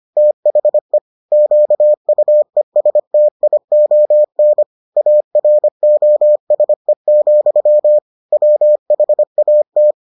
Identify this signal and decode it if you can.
Morse code, heard as a keyed tone, about 25 wpm, 600 Hz. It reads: THE QUESTION AROSE, WHAT